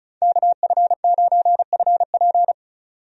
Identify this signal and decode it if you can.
Morse code, heard as a keyed tone, about 34 wpm, 700 Hz. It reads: KF9FP
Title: USA Call Signs - 5 Characters - 35wpm